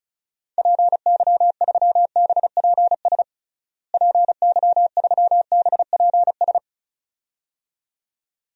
Morse code, heard as a keyed tone, about 36 wpm, 700 Hz. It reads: PY3BPS PY3BPS
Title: DXCC-100 Call Signs - 6 Characters - Repeated 2x - 35wpm